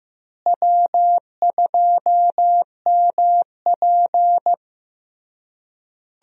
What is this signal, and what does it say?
Morse code, heard as a keyed tone, about 16 wpm, 700 Hz. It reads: W2MP
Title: USA Call Signs - 4 Characters - Repeated 3x - 15wpm